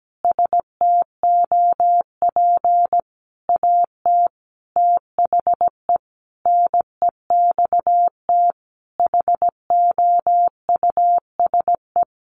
Morse code, heard as a keyed tone, about 17 wpm, 700 Hz. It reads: STOP AT THE NEXT HOUSE
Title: Sentences from Top 400 Words 17wpm